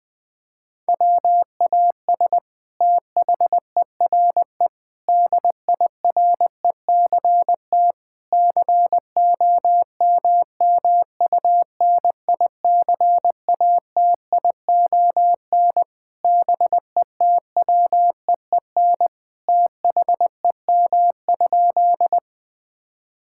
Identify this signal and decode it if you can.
Morse code, heard as a keyed tone, about 20 wpm, 700 Hz. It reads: WAS THERE DIRECT COMMUNICATION BETWEEN THEM?
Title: Sentences from Top 1000 Words - Review Part 6 - 20wpm